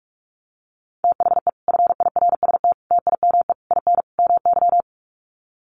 Morse code, heard as a keyed tone, about 44 wpm, 700 Hz. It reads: THE FIRST TIME IN MY
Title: Sets of 5 Words Encore 45wpm